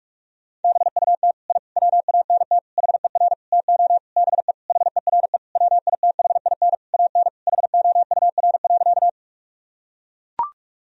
Morse code, heard as a keyed tone, about 42 wpm, 700 Hz. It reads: BUT I WANT HER TO BE HERE WITHIN AN HOUR.
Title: Sentences from Top 1000 Words - Review Part 4 - 45wpm